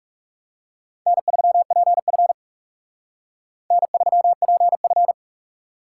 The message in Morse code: N3PF N3PF

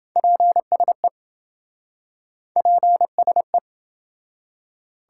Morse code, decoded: PSE PSE

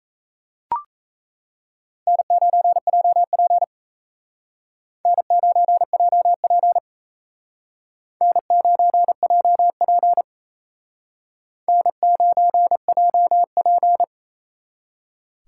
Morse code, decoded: N9JP N9JP N9JP N9JP

